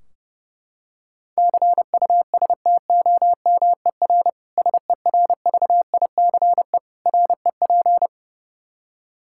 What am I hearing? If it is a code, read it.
Morse code, heard as a keyed tone, about 30 wpm, 700 Hz. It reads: CUSTOMER SERVICE REP
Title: Practice QSO - Exchange 3 - More Info - 30wpm